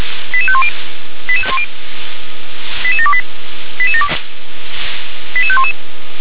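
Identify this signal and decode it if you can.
Five-tone selective call (ZVEI), heard as quick runs of five tones, tones of about 70 ms. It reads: 88510 88728 88511
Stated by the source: ZVEI five-tone selective call 88743 received with an SDR.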